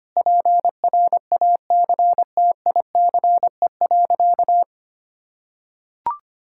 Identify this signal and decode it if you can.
Morse code, heard as a keyed tone, about 25 wpm, 700 Hz. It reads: PRACTICE.
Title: Sentences from Top 1000 Words - Review Part 5 - 25wpm